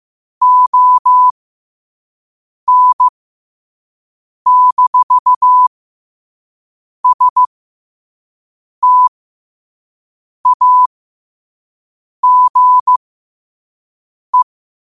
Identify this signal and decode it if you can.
Morse code, heard as a keyed tone, about 15 wpm, spaced as for 5 wpm, 1000 Hz. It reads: ON-STAGE
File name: qotdfiles / 5wpm